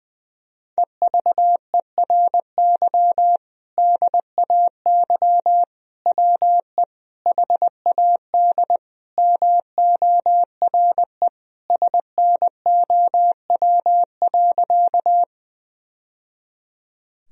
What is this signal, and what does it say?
Morse code, heard as a keyed tone, about 20 wpm, 700 Hz. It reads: EVERY DAY WE HAD MORE SNOW.
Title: Sentences from Top 1000 Words - Review Part 3 - 20wpm